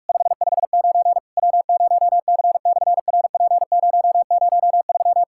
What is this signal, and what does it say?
Morse code, heard as a keyed tone, about 46 wpm, 700 Hz. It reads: 559 W0C/RP003
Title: SOTA QSO Practice 45wpm